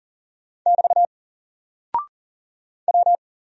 Morse code, heard as a keyed tone, about 40 wpm, 700 Hz. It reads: <BT> W
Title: Single Character (Letter:Number:Punctuation:Prosign) Alt 40wpm